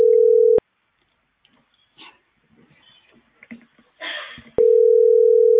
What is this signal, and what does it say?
Ringback tone (North America), 440 + 480 Hz.